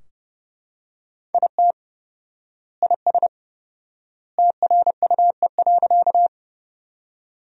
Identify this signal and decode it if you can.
Morse code, heard as a keyed tone, about 32 wpm, 700 Hz. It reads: IT IS TRUE.
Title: 4x Word-Spacing - Sentences from Top 400 Words - 30wpm